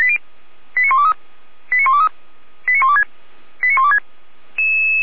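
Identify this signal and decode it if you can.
Five-tone selective call (ZVEI), heard as quick runs of five tones, tones of about 70 ms. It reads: 89123 89123 89126 89126
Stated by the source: ZVEI five-tone call received by SDR.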